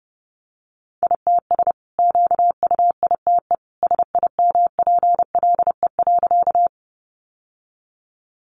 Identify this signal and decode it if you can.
Morse code, heard as a keyed tone, about 30 wpm, 700 Hz. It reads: ITS QUITE SIMPLE.
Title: Sentences from Top 700 Words 30wpm